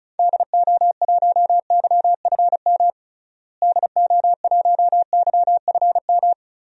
Morse code, heard as a keyed tone, about 36 wpm, 700 Hz. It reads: DO1YFM DO1YFM
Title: DXCC-100 Call Signs - 6 Characters - Repeated 2x - 35wpm